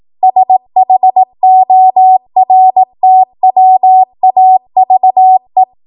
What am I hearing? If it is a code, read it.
Morse code, heard as a keyed tone, about 18 wpm, 750 Hz. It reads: SHORTWAVE
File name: Archive / 18 WPM / 18WPM